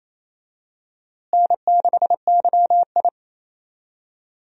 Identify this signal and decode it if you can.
Morse code, heard as a keyed tone, about 27 wpm, 700 Hz. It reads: N6YI
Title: USA Call Signs - 4 Characters - Repeated 2x - 28wpm